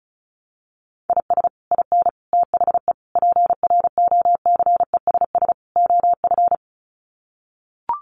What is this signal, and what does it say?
Morse code, heard as a keyed tone, about 34 wpm, 700 Hz. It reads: IS IN THE PROCESS OF
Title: Sets of 5 Words Encore2 35wpm